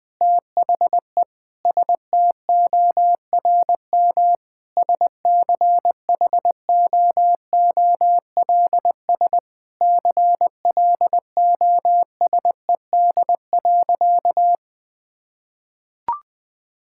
Morse code, heard as a keyed tone, about 21 wpm, 700 Hz. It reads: THE STORM SCHOOLS CLOSED.
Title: Sentences with 5-Word Phrases - 20wpm